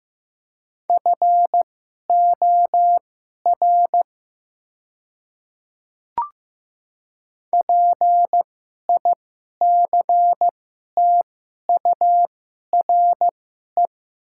Morse code, heard as a keyed tone, about 15 wpm, spaced as for 11 wpm, 700 Hz. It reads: FOR PICTURE
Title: 2x Character Spacing - Top 500 Words - 15wpm